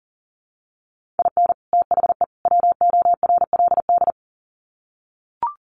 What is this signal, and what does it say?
Morse code, heard as a keyed tone, about 40 wpm, 700 Hz. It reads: IN THE WORLD
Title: Sets of 3 Words 40wpm